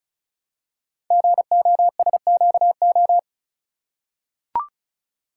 Morse code, heard as a keyed tone, about 36 wpm, 700 Hz. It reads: GOSQO
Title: Code Groups - 5 Letters - 35wpm